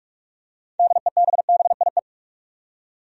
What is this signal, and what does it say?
Morse code, heard as a keyed tone, about 46 wpm, 700 Hz. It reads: DEBBIE
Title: USA Common Names - 45wpm